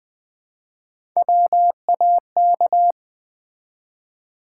Morse code, heard as a keyed tone, about 20 wpm, 700 Hz. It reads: WAK
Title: Build A Word - 1 Letter At A Time - 4-Letter Words - 20wpm